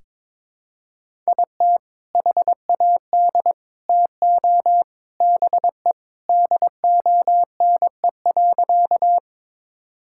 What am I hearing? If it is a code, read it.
Morse code, heard as a keyed tone, about 22 wpm, 700 Hz. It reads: IT HAD TO BE DONE.
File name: practice / 22wpm / Sentences with 3-Word Phrases - 22wpm